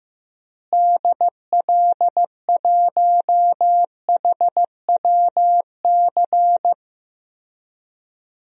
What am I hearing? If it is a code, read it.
Morse code, heard as a keyed tone, about 15 wpm, 700 Hz. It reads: DL1HWC
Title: DXCC-100 Call Signs - 6 Characters - Repeated 3x - 15wpm